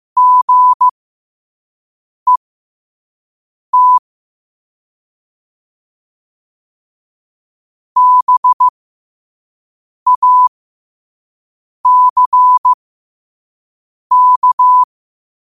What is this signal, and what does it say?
Morse code, heard as a keyed tone, about 15 wpm, spaced as for 5 wpm, 1000 Hz. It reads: GET BACK
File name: qotdfiles / 5wpm